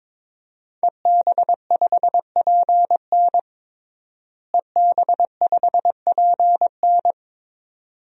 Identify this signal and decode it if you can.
Morse code, heard as a keyed tone, about 22 wpm, 700 Hz. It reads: EB5PN EB5PN
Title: DXCC-100 Call Signs - 5 Characters - Repeated 2x - 22wpm